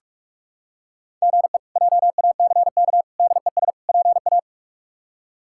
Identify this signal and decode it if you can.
Morse code, heard as a keyed tone, about 41 wpm, 700 Hz. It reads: GE JACK DES PA
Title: K1USN SST QSO Practice - 45wpm